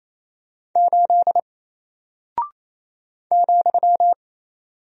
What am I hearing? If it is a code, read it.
Morse code, heard as a keyed tone, about 27 wpm, 700 Hz. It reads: 8 ,
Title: Single Character (Letter:Number:Punctuation:Prosign) 28wpm